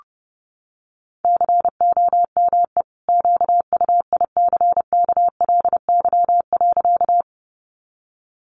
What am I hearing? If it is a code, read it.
Morse code, heard as a keyed tone, about 30 wpm, 700 Hz. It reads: COME QUICKLY.
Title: Sentences from Top 700 Words 30wpm